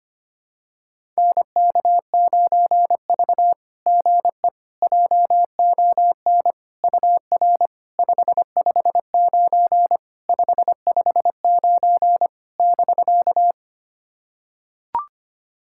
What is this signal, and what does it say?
Morse code, heard as a keyed tone, about 25 wpm, 700 Hz. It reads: NK9V GE JON UR 559 559 <BK>
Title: SOTA QSO Practice 25wpm